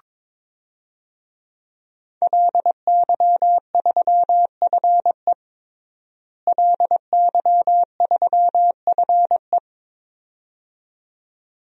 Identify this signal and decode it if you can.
Morse code, heard as a keyed tone, about 22 wpm, 700 Hz. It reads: LY3FE LY3FE
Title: DXCC-100 Call Signs - 5 Characters - Repeated 2x - 22wpm